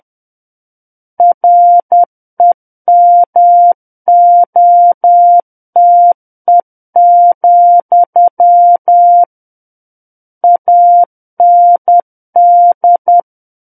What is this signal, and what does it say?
Morse code, heard as a keyed tone, about 10 wpm, 700 Hz. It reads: REMOTE, AND
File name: Book One / 01-Chapter 1 - 10 WPM